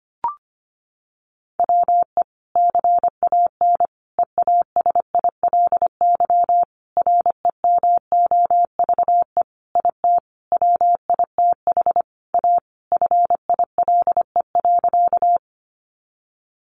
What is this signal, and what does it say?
Morse code, heard as a keyed tone, about 25 wpm, 700 Hz. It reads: WE CAN EASILY REMOVE IT WITH A FILE.
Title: Sentences from Top 1200 Words 25wpm